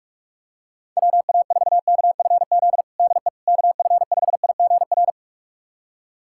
Morse code, heard as a keyed tone, about 45 wpm, 700 Hz. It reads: WA4KFZ DE KF5IGR
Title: USA Callsign DE Callsign - 6 Characters - 45wpm